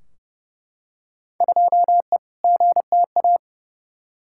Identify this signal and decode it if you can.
Morse code, heard as a keyed tone, about 31 wpm, 700 Hz. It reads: 2E GTA
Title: Field Day Exchanges Encore - 30wpm